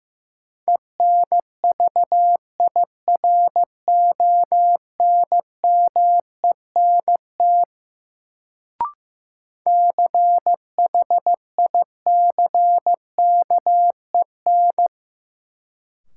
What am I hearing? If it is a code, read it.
Morse code, heard as a keyed tone, about 15 wpm, 700 Hz. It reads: ENVIRONMENT CHICKEN
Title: Sentences from Top 1000 Words - Review Part 5 - New Word Drill - 15wpm